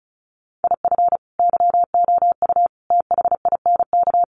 Morse code, heard as a keyed tone, about 34 wpm, 700 Hz. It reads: IF YOU THINK
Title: Sets of 5 Words Encore2 35wpm